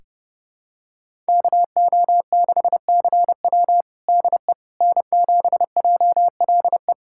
Morse code, heard as a keyed tone, about 30 wpm, 700 Hz. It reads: KO6CW DE N7JLE
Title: USA Callsign DE Callsign - 5 Characters - 30wpm